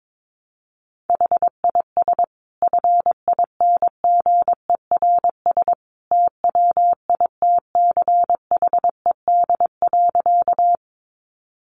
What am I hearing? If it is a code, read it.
Morse code, heard as a keyed tone, about 22 wpm, 700 Hz. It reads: HIS FINGERS TWITCHED.
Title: A Sound Of Thunder 22wpm